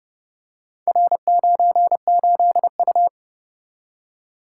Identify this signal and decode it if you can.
Morse code, heard as a keyed tone, about 31 wpm, 700 Hz. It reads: R98U